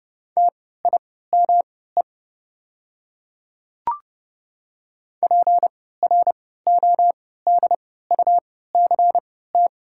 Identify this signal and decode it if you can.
Morse code, heard as a keyed tone, about 31 wpm, spaced as for 17 wpm, 700 Hz. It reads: TIME PRODUCT
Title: 3x Character Spacing - Top 500 Words - 30wpm